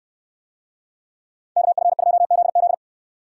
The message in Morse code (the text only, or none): KF2ZZ